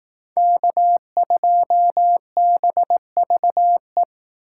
Morse code, heard as a keyed tone, about 18 wpm, 700 Hz. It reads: K2BVE
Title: Speed Racing - USA 1x3 Call Signs 15wpm